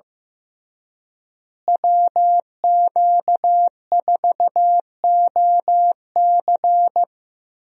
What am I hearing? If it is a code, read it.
Morse code, heard as a keyed tone, about 15 wpm, 700 Hz. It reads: WQ4OC
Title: USA Call Signs - 5 Characters - Repeated 3x - 15wpm